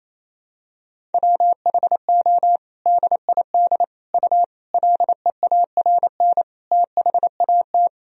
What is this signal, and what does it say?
Morse code, heard as a keyed tone, about 27 wpm, 700 Hz. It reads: WHO DID U LEARN THAT
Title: Sentences from Top 300 Words and Abbreviations 28wpm